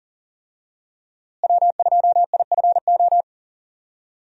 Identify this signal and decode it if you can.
Morse code, heard as a keyed tone, about 37 wpm, 700 Hz. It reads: W2IFO
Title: USA Call Signs - 5 Characters - Repeated 2x - 40wpm